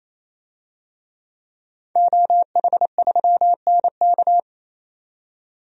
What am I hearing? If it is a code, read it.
Morse code, heard as a keyed tone, about 28 wpm, 700 Hz. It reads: OH3NK
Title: DXCC-100 Call Signs - 5 Characters - 28wpm